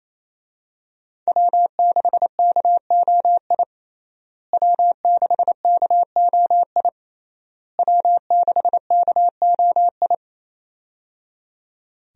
Morse code, heard as a keyed tone, about 28 wpm, 700 Hz. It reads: W6KOI W6KOI W6KOI
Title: USA Call Signs - 5 Characters - Repeated 3x - 28wpm